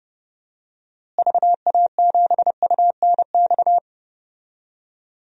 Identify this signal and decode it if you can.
Morse code, heard as a keyed tone, about 32 wpm, 700 Hz. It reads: VA7UNX
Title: DXCC-100 Call Signs - 6 Characters - Repeated 2x - 30wpm